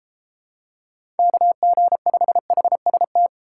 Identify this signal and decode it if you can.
Morse code, heard as a keyed tone, about 29 wpm, 700 Hz. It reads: KG5HST